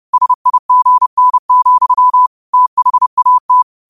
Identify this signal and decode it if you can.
Morse code, heard as a keyed tone, about 30 wpm, 1000 Hz. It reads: SIGN, THAT